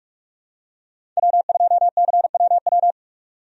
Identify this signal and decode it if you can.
Morse code, heard as a keyed tone, about 46 wpm, 700 Hz. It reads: W2CWW
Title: USA Call Signs - 5 Characters - 45wpm